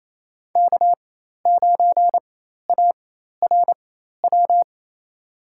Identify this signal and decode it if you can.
Morse code, heard as a keyed tone, about 28 wpm, spaced as for 13 wpm, 700 Hz. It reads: K9ARW
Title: USA Call Signs - 4x Character Spacing - 28wpm